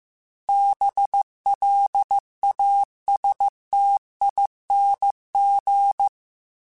Morse code, heard as a keyed tone, about 15 wpm, 800 Hz. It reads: BLASTING
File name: Special INGs-mp3s / Special INGs-mp3s / Sngl ing @13